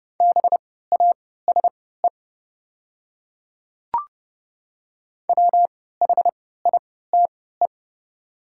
Morse code, heard as a keyed tone, about 30 wpm, spaced as for 17 wpm, 700 Hz. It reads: BASE WHITE
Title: 3x Character Spacing - Top 500 Words - 30wpm